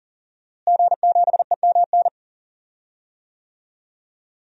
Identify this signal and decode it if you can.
Morse code, heard as a keyed tone, about 37 wpm, 700 Hz. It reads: G7EMN
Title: DXCC-100 Call Signs - 5 Characters - Repeated 3x - 40wpm